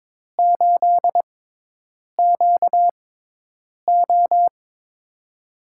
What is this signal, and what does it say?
Morse code, heard as a keyed tone, about 22 wpm, spaced as for 8 wpm, 700 Hz. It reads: 8QO
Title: USA Call Signs - 6x Character Spacing - 22wpm